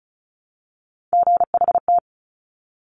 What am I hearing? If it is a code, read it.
Morse code, heard as a keyed tone, about 36 wpm, 700 Hz. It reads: GHT
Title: Top Three Letter Combinations in Words 35wpm